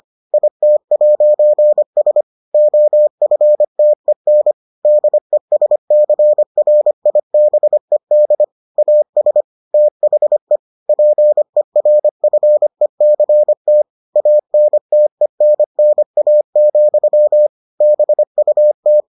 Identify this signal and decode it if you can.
Morse code, heard as a keyed tone, about 25 wpm, 600 Hz. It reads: IT'S OFTEN DESCRIBED AS THE PERFECT ANTENNA, BUT